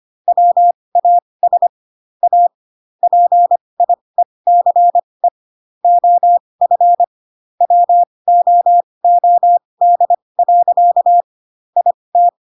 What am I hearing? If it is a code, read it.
Morse code, heard as a keyed tone, about 25 wpm, spaced as for 20 wpm, 700 Hz. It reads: WAS A PIECE OF WOOD. IT
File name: Pinocchio / Pinocchio ch1 15 25 ik1hge